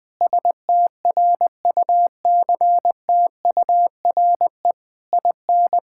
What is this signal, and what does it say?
Morse code, heard as a keyed tone, about 20 wpm, 700 Hz. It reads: STRUCTURE IN
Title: Sentences from Top 1000 Words - Review Part 6 - 20wpm